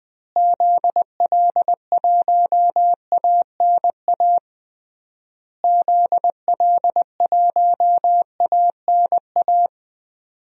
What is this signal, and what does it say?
Morse code, heard as a keyed tone, about 20 wpm, 700 Hz. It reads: ZL1ANA ZL1ANA
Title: DXCC-100 Call Signs - 6 Characters - Repeated 3x - 20wpm